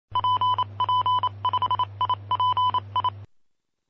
Morse code, heard as a keyed tone, about 28 wpm, 1000 Hz. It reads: PP5IPI
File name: aulas de telegrafia